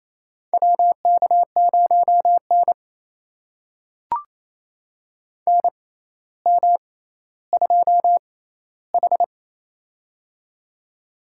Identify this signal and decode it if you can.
Morse code, heard as a keyed tone, about 27 wpm, spaced as for 14 wpm, 700 Hz. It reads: WK0N NM2H
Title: USA Call Signs - 4 Characters - 6x Character Spacing - 28wpm